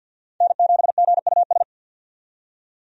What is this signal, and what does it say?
Morse code, heard as a keyed tone, about 52 wpm, 700 Hz. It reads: N7GUS
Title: USA Call Signs - 5 Characters - 50wpm